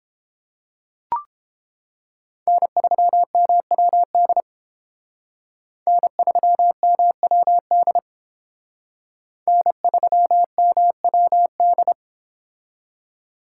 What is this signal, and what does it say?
Morse code, heard as a keyed tone, about 29 wpm, 700 Hz. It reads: N3MWD N3MWD N3MWD